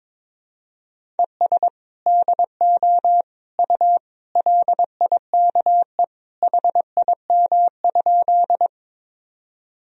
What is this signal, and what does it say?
Morse code, heard as a keyed tone, about 22 wpm, 700 Hz. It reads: ES DO U LIKE HIM?
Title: Sentences from Top 100 Words and Abbreviations 22wpm